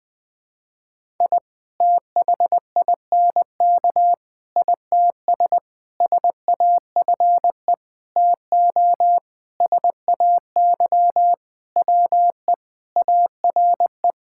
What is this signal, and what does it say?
Morse code, heard as a keyed tone, about 20 wpm, 700 Hz. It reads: I THINK ITS SAFE TO SAY WE ARE
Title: Sentences with 6-Word Phrases - Spoken Beginning - 20wpm